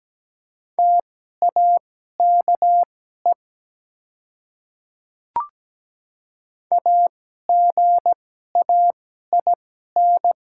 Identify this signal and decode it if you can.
Morse code, heard as a keyed tone, about 17 wpm, spaced as for 12 wpm, 700 Hz. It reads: TAKE AGAIN
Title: 2x Character Spacing - Top 300 Words - 17wpm